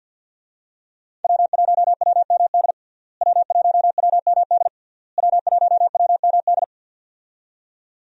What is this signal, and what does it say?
Morse code, heard as a keyed tone, about 52 wpm, 700 Hz. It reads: W1WMD W1WMD W1WMD
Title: USA Call Signs - 5 Characters - Repeated 3x - 50wpm